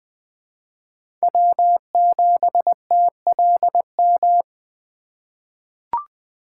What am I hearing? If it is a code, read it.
Morse code, heard as a keyed tone, about 20 wpm, 700 Hz. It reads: W7TLM
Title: USA Call Signs - 5 Characters - 4x Character Spacing - 20wpm